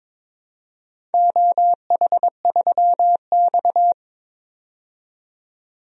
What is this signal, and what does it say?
Morse code, heard as a keyed tone, about 22 wpm, 700 Hz. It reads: OH3X
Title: DXCC-100 Call Signs - 4 Characters - 22wpm